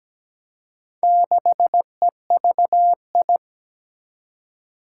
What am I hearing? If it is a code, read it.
Morse code, heard as a keyed tone, about 17 wpm, 700 Hz. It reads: 6EVI
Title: Four Challenging Letter-Number Combinations 17wpm